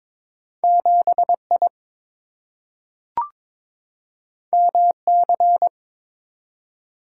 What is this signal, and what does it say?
Morse code, heard as a keyed tone, about 22 wpm, 700 Hz. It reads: 7I MC
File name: practice / 22wpm / Two Letter-Number Combinations - 22wpm